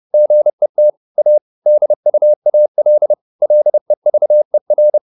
Morse code, heard as a keyed tone, about 30 wpm, 600 Hz. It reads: GET A DUAL LEVER
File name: SS301 30